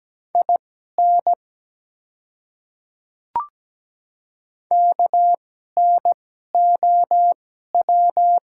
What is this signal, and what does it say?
Morse code, heard as a keyed tone, about 17 wpm, spaced as for 12 wpm, 700 Hz. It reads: IN KNOW
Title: 2x Character Spacing - Top 100 Words - 17wpm